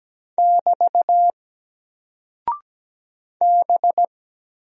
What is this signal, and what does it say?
Morse code, heard as a keyed tone, about 17 wpm, 700 Hz. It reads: <BT> B
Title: Single Character (Letter:Number:Punctuation:Prosign) Alt 17wpm